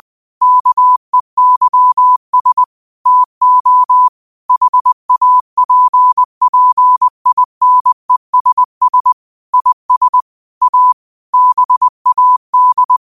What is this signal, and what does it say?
Morse code, heard as a keyed tone, about 20 wpm, 1000 Hz. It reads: KEYS TO HAPPINESS IS A BAD
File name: qotdfiles / 20wpm